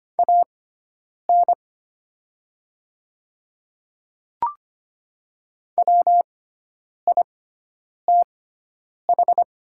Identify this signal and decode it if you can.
Morse code, heard as a keyed tone, about 25 wpm, spaced as for 9 wpm, 690 Hz. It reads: AN WITH
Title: 6x Character Spacing - Top 100 Words - 25wpm